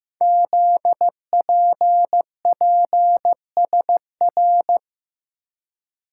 Morse code, heard as a keyed tone, about 15 wpm, 700 Hz. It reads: ZPPSR
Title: Code Groups - 5 Letters - 15wpm